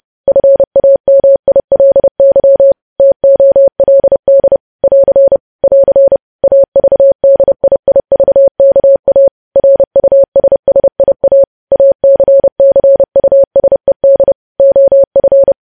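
Morse code, heard as a keyed tone, about 30 wpm, 550 Hz. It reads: FAMILY TOLD <AR> <AR> AVDIIVKA RUSSIA ACCUSED OF
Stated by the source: Morse code news headlines sent at 30 wpm.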